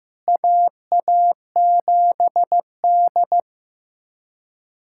Morse code, heard as a keyed tone, about 15 wpm, 700 Hz. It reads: AA7D
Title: USA Callsign DE Callsign - 4 Characters - 15wpm